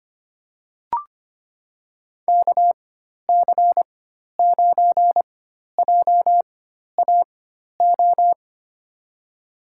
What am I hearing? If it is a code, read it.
Morse code, heard as a keyed tone, about 25 wpm, spaced as for 12 wpm, 700 Hz. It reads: KC9JAO
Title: USA Call Signs - 6 Characters - 4x Character Spacing - 25wpm